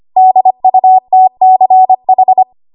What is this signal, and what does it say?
Morse code, heard as a keyed tone, about 25 wpm, 750 Hz. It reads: DUTCH
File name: Archive / 25 WPM / 25WPM